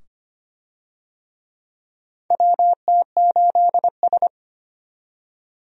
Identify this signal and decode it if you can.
Morse code, heard as a keyed tone, about 25 wpm, 700 Hz. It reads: WT8S
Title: USA Call Signs - 4 Characters - 25wpm